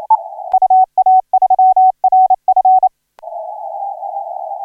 Morse code, heard as a keyed tone, about 30 wpm, 750 Hz. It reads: UA3RF
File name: RW3RN HI7 1835UTC 12m CW